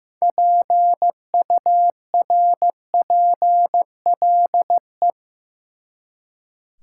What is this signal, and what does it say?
Morse code, heard as a keyed tone, about 15 wpm, 700 Hz. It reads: PURPLE
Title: Six Letter Words - 15wpm